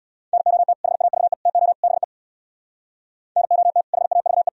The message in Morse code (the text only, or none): NOT HAVE TO BE NOT HAVE